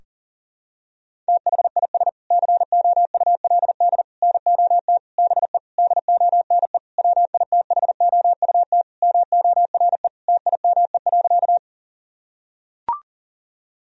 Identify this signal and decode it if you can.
Morse code, heard as a keyed tone, about 40 wpm, 700 Hz. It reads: THIS COULD NOT BE DONE WITHOUT MORE TIME.
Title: Sentences from Top 1000 Words - Review Part 3 - 40wpm